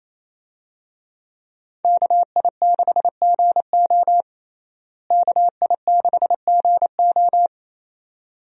Morse code, heard as a keyed tone, about 27 wpm, 700 Hz. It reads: KI6GO KI6GO
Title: USA Call Signs - 5 Characters - Repeated 2x - 28wpm